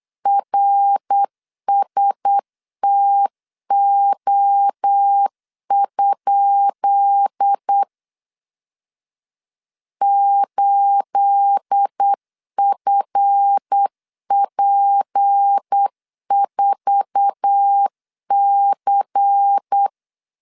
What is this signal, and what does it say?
Morse code, heard as a keyed tone, about 8 wpm, 800 Hz. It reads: RSTO? 8FP4C
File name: LEZIONI / 10 SIGNS / 10B INT 8wpm 3534